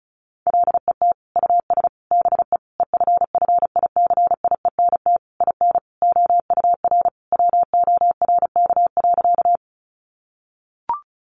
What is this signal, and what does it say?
Morse code, heard as a keyed tone, about 34 wpm, 700 Hz. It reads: LET US BE EFFICIENT IN OUR WORK.
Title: Sentences from Top 1300 Words 35wpm